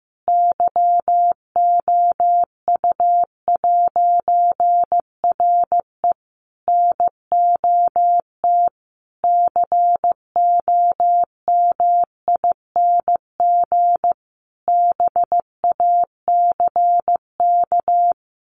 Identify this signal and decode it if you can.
Morse code, heard as a keyed tone, about 15 wpm, 700 Hz. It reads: YOU'RE NOT COMING BACK
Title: A Sound Of Thunder 15wpm